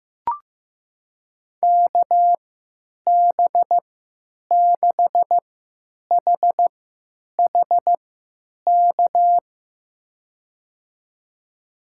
Morse code, heard as a keyed tone, about 15 wpm, spaced as for 9 wpm, 700 Hz. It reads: KB6HHK
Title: USA Call Signs - 6 Characters - 3x Character Spacing - 15wpm